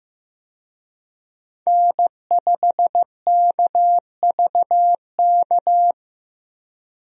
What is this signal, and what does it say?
Morse code, heard as a keyed tone, about 15 wpm, 700 Hz. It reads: N5KVK
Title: USA Call Signs - 5 Characters - Repeated 3x - 15wpm